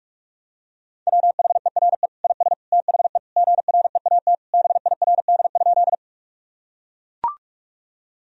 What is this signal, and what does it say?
Morse code, heard as a keyed tone, about 42 wpm, 700 Hz. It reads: WHERE IS THE GREAT BIRD?
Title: Sentences from Top 1000 Words - Review Part 2 - 45wpm